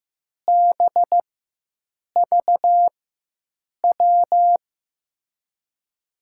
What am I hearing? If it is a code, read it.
Morse code, heard as a keyed tone, about 15 wpm, spaced as for 7 wpm, 700 Hz. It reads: BVW
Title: USA Call Signs - 6 Characters - 4x Character Spacing - 15wpm